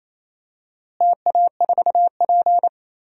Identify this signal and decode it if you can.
Morse code, heard as a keyed tone, about 28 wpm, 700 Hz. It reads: TA4P
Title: DXCC-100 Call Signs - 4 Characters - Repeated 2x - 28wpm